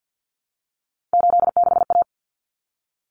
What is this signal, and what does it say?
Morse code, heard as a keyed tone, about 50 wpm, 700 Hz. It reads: 86A
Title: Common Cut Numbers - 3-Digit 50wpm